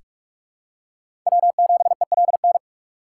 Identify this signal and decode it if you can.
Morse code, heard as a keyed tone, about 46 wpm, 700 Hz. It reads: W7ELN
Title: USA Call Signs - 4x Character Spacing - 45wpm